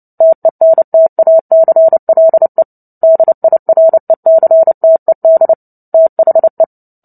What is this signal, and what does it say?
Morse code, heard as a keyed tone, about 29 wpm, 650 Hz. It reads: TENTACLE DIRECTED THE
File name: Book Two / 03-Chapter 3 - 29 WPM